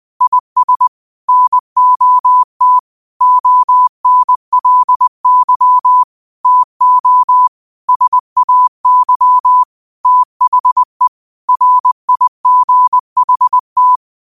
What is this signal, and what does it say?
Morse code, heard as a keyed tone, about 20 wpm, 1000 Hz. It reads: IS NOT ONLY TO SAY THE RIGHT